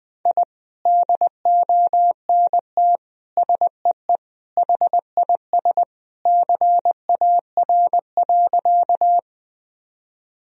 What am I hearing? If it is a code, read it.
Morse code, heard as a keyed tone, about 20 wpm, 700 Hz. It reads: I DONT SEE HIS CAR.
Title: Sentences from Top 1000 Words - Review Part 1 - 20wpm